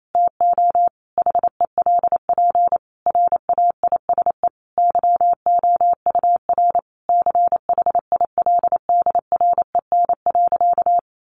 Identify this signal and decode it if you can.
Morse code, heard as a keyed tone, about 28 wpm, 700 Hz. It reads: TO HELP RAISE YOUR CHILDREN.
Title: Sentences from Top 1400 Words 28wpm